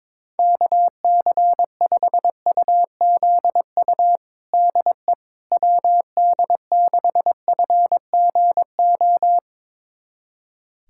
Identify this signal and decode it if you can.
Morse code, heard as a keyed tone, about 22 wpm, 700 Hz. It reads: KC5UZU DE WD6FGO
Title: USA Callsign DE Callsign 22wpm